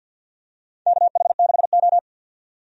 Morse code, heard as a keyed tone, about 51 wpm, 700 Hz. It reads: KH6O
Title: DXCC-100 Call Signs - 4 Characters - 50wpm